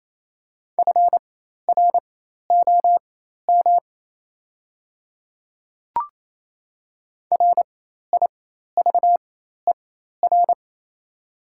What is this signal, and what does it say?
Morse code, heard as a keyed tone, about 28 wpm, spaced as for 13 wpm, 690 Hz. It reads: FROM RIVER
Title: 4x Character Spacing - Top 300 Words - 28wpm